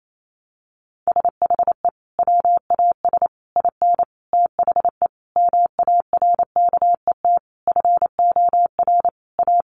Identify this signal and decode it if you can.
Morse code, heard as a keyed tone, about 28 wpm, 700 Hz. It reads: SHE WAS IN THE MARKET FOR A
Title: Sentences from Top 600 Words 28wpm